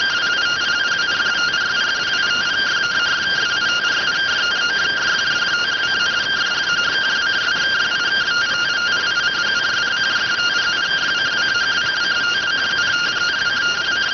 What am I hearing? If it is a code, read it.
Two-tone teleprinter signal, about 170 Hz shift.